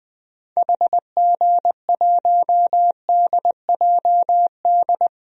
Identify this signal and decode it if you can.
Morse code, heard as a keyed tone, about 20 wpm, 700 Hz. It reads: HG1DJD
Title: DXCC-100 Call Signs - 6 Characters - Repeated 3x - 20wpm